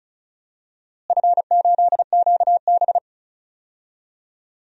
Morse code, heard as a keyed tone, about 34 wpm, 700 Hz. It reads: F8QB
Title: DXCC-100 Call Signs - 4 Characters - Repeated 3x - 35wpm